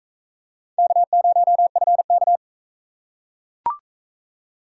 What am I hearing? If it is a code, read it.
Morse code, heard as a keyed tone, about 40 wpm, 700 Hz. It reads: K0FK